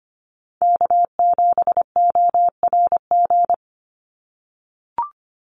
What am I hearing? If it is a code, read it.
Morse code, heard as a keyed tone, about 25 wpm, 700 Hz. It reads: K7ORG
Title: NA SOTA Chaser Callsigns 25wpm